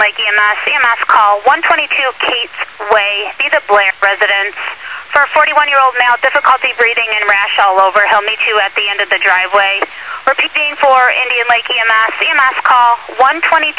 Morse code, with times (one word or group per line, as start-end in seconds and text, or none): none